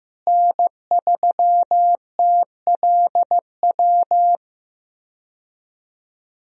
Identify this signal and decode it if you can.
Morse code, heard as a keyed tone, about 15 wpm, 700 Hz. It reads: N3TLW
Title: USA Call Signs - 5 Characters - Repeated 3x - 15wpm